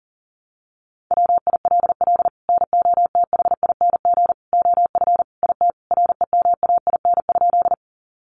Morse code, heard as a keyed tone, about 37 wpm, 700 Hz. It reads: WILL NOTHING OF IT REMAIN?
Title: Sentences from Top 1200 Words 40wpm